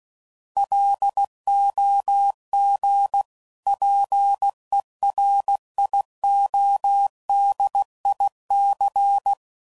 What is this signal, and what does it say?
Morse code, heard as a keyed tone, about 16 wpm, 800 Hz. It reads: LOGPERIODIC
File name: CW Code Groups @15WPM / fromSM's / Antennas